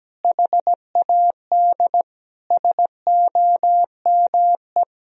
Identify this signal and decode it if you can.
Morse code, heard as a keyed tone, about 17 wpm, 700 Hz. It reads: HAD SOME
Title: Sentences from Top 1000 Words - Review Part 4 - 17wpm